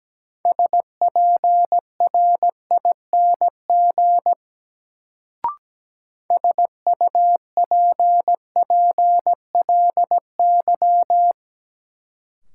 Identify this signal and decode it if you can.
Morse code, heard as a keyed tone, about 17 wpm, 700 Hz. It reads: SPRING SUPPLY
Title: Sentences from Top 1200 Words - New Word Drill Alt - 17wpm